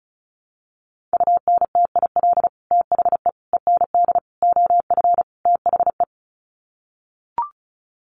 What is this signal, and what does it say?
Morse code, heard as a keyed tone, about 34 wpm, 700 Hz. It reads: UNTIL THE END OF THE
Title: Sets of 5 Words Encore 35wpm